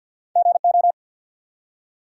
Morse code, heard as a keyed tone, about 50 wpm, 700 Hz. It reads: GO
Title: Two Letter Words - 50wpm